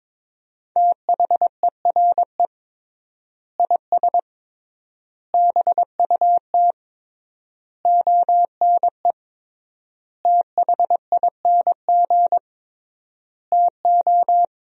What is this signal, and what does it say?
Morse code, heard as a keyed tone, about 22 wpm, 700 Hz. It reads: THERE IS BUT ONE THING TO
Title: 3x Word-Spacing - Sentences from Top 100 Words - 22wpm